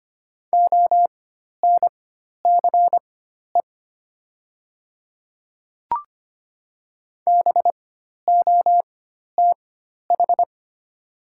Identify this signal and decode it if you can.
Morse code, heard as a keyed tone, about 25 wpm, spaced as for 12 wpm, 700 Hz. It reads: ONCE BOTH
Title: 4x Character Spacing - Top 300 Words - 25wpm